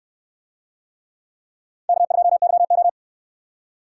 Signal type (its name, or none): Morse code